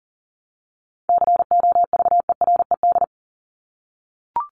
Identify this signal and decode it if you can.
Morse code, heard as a keyed tone, about 39 wpm, 700 Hz. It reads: COVERED
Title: ED Suffix - Top 100 Words with Base Words 40wpm